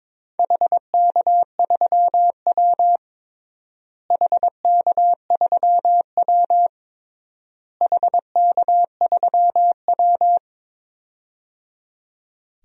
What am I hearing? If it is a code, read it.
Morse code, heard as a keyed tone, about 21 wpm, 700 Hz. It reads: HK3W HK3W HK3W
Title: DXCC-100 Call Signs - 4 Characters - Repeated 3x - 22wpm